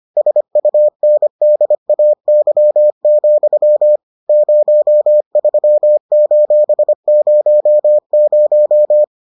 Morse code, heard as a keyed tone, about 25 wpm, 600 Hz. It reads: SUNDAY, 03:00